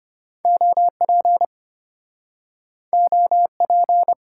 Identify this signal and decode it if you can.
Morse code, heard as a keyed tone, about 33 wpm, 700 Hz. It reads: OP OP